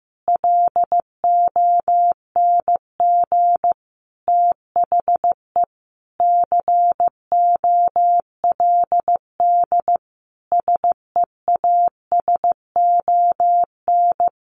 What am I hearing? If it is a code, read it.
Morse code, heard as a keyed tone, about 15 wpm, 700 Hz. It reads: LONG THE COLD SEASON
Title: Sentences from Top 1400 Words 15wpm